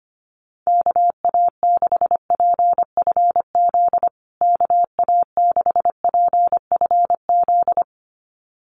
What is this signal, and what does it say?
Morse code, heard as a keyed tone, about 25 wpm, 700 Hz. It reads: KA6PFZ KA6PFZ
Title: USA Call Signs Repeated 2x 25wpm